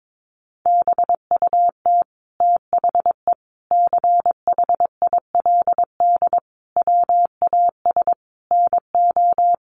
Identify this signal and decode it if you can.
Morse code, heard as a keyed tone, about 22 wpm, 700 Hz. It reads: BUT THE CHILD WAS NO
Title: Sentences from Top 1500 Words 22wpm